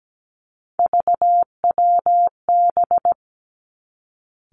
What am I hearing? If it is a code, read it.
Morse code, heard as a keyed tone, about 17 wpm, 700 Hz. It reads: VWB
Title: Three Challenging Letter-Number Combinations 17wpm